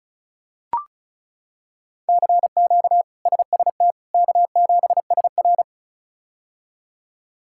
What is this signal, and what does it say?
Morse code, heard as a keyed tone, about 36 wpm, 700 Hz. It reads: CQ SST K7SR
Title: K1USN SST QSO Practice - 35wpm